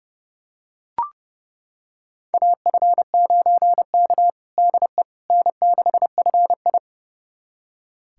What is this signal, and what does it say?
Morse code, heard as a keyed tone, about 30 wpm, 700 Hz. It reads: AF9K DE N6FI
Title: USA Callsign DE Callsign - 4 Characters - 30wpm